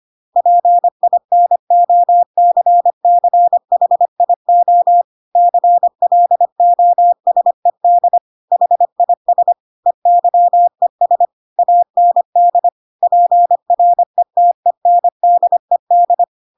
Morse code, heard as a keyed tone, about 25 wpm, 700 Hz. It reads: PINOCCHIO CLOSED HIS EYES AND PRETENDED